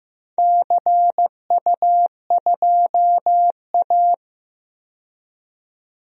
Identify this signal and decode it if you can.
Morse code, heard as a keyed tone, about 15 wpm, 700 Hz. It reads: CU2A
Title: DXCC-100 Call Signs - 4 Characters - 15wpm